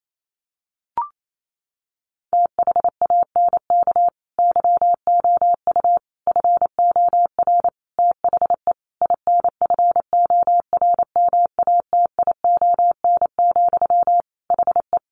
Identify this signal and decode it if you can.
Morse code, heard as a keyed tone, about 28 wpm, 700 Hz. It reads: THANK YOU FOR THE INFORMATION, HE
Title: Sentences from Top 600 Words 28wpm